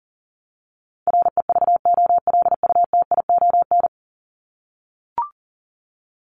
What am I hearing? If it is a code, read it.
Morse code, heard as a keyed tone, about 40 wpm, 700 Hz. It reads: REVOLUTION
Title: Top 1751-2000 Words 40wpm